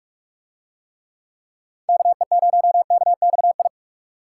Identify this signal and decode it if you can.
Morse code, heard as a keyed tone, about 40 wpm, 700 Hz. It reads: KE0KXI